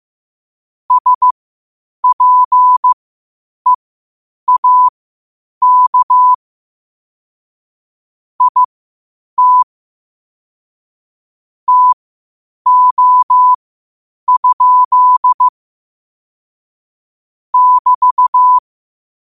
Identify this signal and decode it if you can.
Morse code, heard as a keyed tone, about 15 wpm, spaced as for 8 wpm, 1000 Hz. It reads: SPEAK IT TO? <BT>
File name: qotdfiles / 8wpm